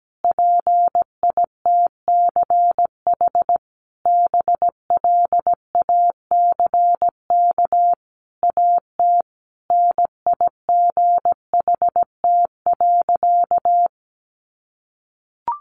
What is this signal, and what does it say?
Morse code, heard as a keyed tone, about 17 wpm, 700 Hz. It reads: PITCH BLACK AT NIGHT.
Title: Sentences from Top 1300 Words 17wpm